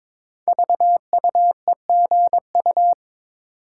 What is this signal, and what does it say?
Morse code, heard as a keyed tone, about 21 wpm, 700 Hz. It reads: VUEGU
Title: Code Groups - 5 Letters - 22wpm